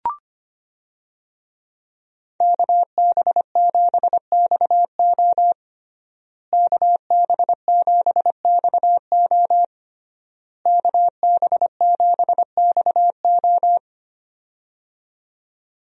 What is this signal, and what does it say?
Morse code, heard as a keyed tone, about 25 wpm, 700 Hz. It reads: KB7XO KB7XO KB7XO
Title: USA Call Signs - 5 Characters - Repeated 3x - 25wpm